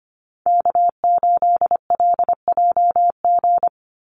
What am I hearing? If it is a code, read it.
Morse code, heard as a keyed tone, about 25 wpm, 700 Hz. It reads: K8LJG
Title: NA SOTA Chaser Callsigns 25wpm